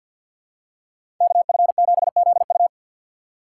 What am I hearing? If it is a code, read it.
Morse code, heard as a keyed tone, about 51 wpm, 700 Hz. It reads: KF7ZU
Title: USA Call Signs - 4x Character Spacing - 50wpm